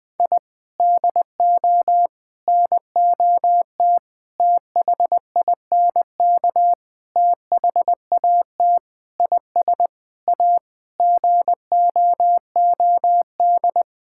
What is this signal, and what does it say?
Morse code, heard as a keyed tone, about 20 wpm, 700 Hz. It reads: I DO NOT THINK THAT IS A GOOD
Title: Sentences with 5-Word Phrases - Spoken Beginning - 20wpm